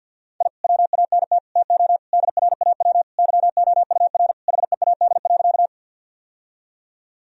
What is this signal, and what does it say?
Morse code, heard as a keyed tone, about 50 wpm, 700 Hz. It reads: I WANT TO DRAW YOUR HEAD.
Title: Sentences from Top 1000 Words - Review Part 1 - 50wpm